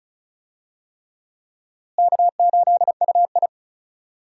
Morse code, heard as a keyed tone, about 36 wpm, 700 Hz. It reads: K8UI
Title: USA Call Signs - 4 Characters - Repeated 3x - 35wpm